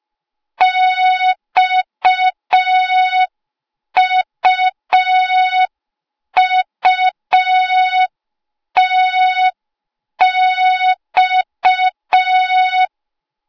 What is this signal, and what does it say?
Morse code, heard as a keyed tone, about 5 wpm, 3700 Hz. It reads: XUUTX